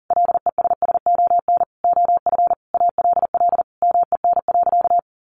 Morse code, heard as a keyed tone, about 40 wpm, 700 Hz. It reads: LESSON OF ALL MEN.
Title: Sentences from Top 1200 Words 40wpm